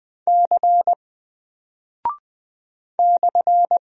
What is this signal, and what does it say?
Morse code, heard as a keyed tone, about 20 wpm, 700 Hz. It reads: C /
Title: Single Character (Letter:Number:Punctuation:Prosign) 20wpm